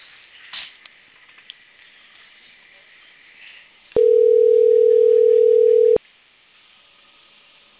Ringback tone (North America), 440 + 480 Hz.